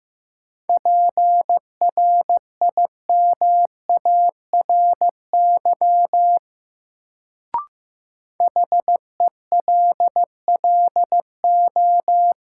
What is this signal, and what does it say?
Morse code, heard as a keyed tone, about 16 wpm, 700 Hz. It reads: PRIMARY HELLO
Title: Sentences from Top 1100 Words - New Word Drill Alt - 15wpm